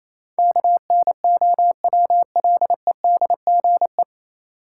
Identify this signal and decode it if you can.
Morse code, heard as a keyed tone, about 28 wpm, 700 Hz. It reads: KNOWLEDGE
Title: Sentences from Top 600 Words - New Word Drill - 28wpm